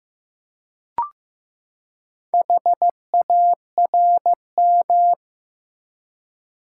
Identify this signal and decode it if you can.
Morse code, heard as a keyed tone, about 15 wpm, 700 Hz. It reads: HARM
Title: Four Letter Words - 15wpm